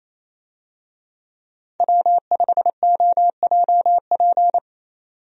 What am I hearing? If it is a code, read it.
Morse code, heard as a keyed tone, about 28 wpm, 700 Hz. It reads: W5OJP
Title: USA Call Signs - 5 Characters - 28wpm